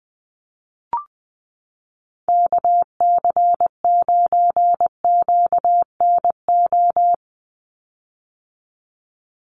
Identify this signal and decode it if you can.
Morse code, heard as a keyed tone, about 20 wpm, 700 Hz. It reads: KC9QNO
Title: USA Call Signs 20wpm